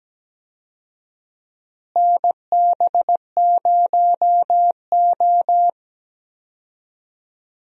Morse code, heard as a keyed tone, about 17 wpm, 700 Hz. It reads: NB0O
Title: USA Call Signs - 4 Characters - Repeated 2x - 17wpm